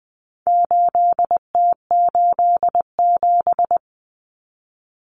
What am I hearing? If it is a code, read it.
Morse code, heard as a keyed tone, about 20 wpm, 700 Hz. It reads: 8T87
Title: Common Cut Numbers - 4-Digit 20wpm